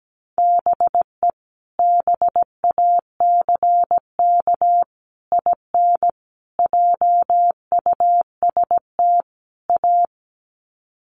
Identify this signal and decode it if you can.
Morse code, heard as a keyed tone, about 17 wpm, 700 Hz. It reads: BE BACK IN JUST A
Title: Sets of 5 Words Encore 17wpm